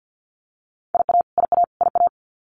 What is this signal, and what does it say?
Morse code, heard as a keyed tone, about 48 wpm, 700 Hz. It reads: IA IA IA
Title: US States Repeated 3x 50wpm